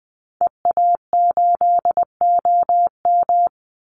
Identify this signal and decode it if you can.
Morse code, heard as a keyed tone, about 20 wpm, 700 Hz. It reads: EA8OM
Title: Call Signs Repeated 2x - 20wpm